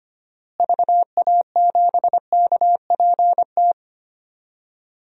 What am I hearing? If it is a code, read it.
Morse code, heard as a keyed tone, about 25 wpm, 700 Hz. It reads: VA7KPT
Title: DXCC-100 Call Signs - 6 Characters - Repeated 2x - 25wpm